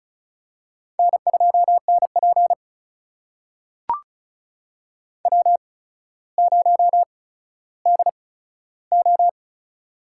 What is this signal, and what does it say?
Morse code, heard as a keyed tone, about 34 wpm, spaced as for 15 wpm, 700 Hz. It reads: N2NP W0DO
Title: USA Call Signs - 4 Characters - 8x Character Spacing - 35wpm